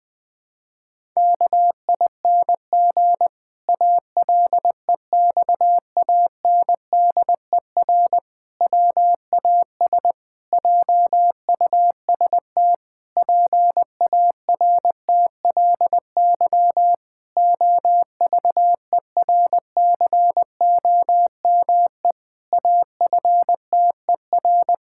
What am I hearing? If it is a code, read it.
Morse code, heard as a keyed tone, about 20 wpm, 700 Hz. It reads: KING ALEXANDER WAS JUST PARTLY OVERCOME AFTER
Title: Pangram Challenge v2 - 20wpm